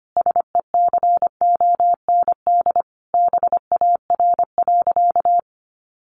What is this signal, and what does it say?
Morse code, heard as a keyed tone, about 25 wpm, 700 Hz. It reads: SECOND BAR.
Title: Easy Phrases 25wpm